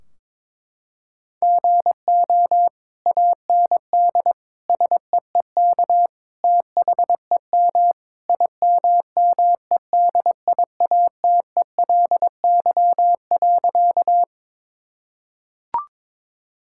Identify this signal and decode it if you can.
Morse code, heard as a keyed tone, about 22 wpm, 700 Hz. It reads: GO AND SEEK THEM IMMEDIATELY.
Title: Sentences from Top 1600 Words 22wpm